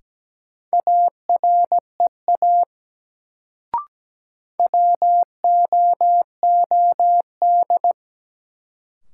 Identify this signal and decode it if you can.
Morse code, heard as a keyed tone, about 17 wpm, 700 Hz. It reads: AREA WOOD
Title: Sentences from Top 1000 Words - Review Part 2 - New Word Drill - 17wpm